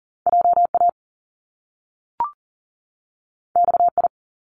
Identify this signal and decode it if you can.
Morse code, heard as a keyed tone, about 42 wpm, 700 Hz. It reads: JA XI
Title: Top Two Letter Combinations in Words 40wpm